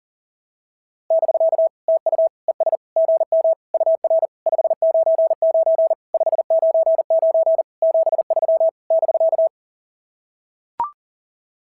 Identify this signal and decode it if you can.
Morse code, heard as a keyed tone, about 40 wpm, 690 Hz. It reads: <BK> TU ES GM UR 599 599 73 <BK>
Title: SOTA QSO Practice 40wpm